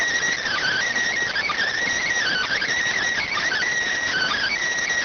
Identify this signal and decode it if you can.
Emergency Alert System (SAME) tones, heard as harsh buzzing bursts. It is heard from 0.4 to 4.5 s.